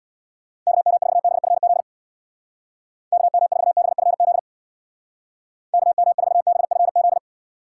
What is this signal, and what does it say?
Morse code, heard as a keyed tone, about 61 wpm, 700 Hz. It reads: KM4BFZ KM4BFZ KM4BFZ
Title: Speed Racing - USA Call Signs 50wpm